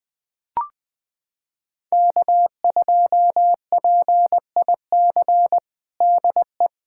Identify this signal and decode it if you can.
Morse code, heard as a keyed tone, about 21 wpm, 700 Hz. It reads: K2PIC DE
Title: USA Callsign DE Callsign 20wpm